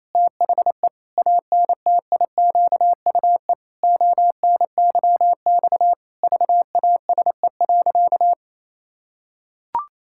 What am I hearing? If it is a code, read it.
Morse code, heard as a keyed tone, about 28 wpm, 700 Hz. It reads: THE ANTIQUE ONYX VASE.
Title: Pangram Challenge v2 - 28wpm